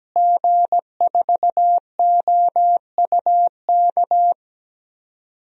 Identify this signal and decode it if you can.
Morse code, heard as a keyed tone, about 17 wpm, 700 Hz. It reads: G4OUK
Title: DXCC-100 Call Signs - 5 Characters - Repeated 3x - 17wpm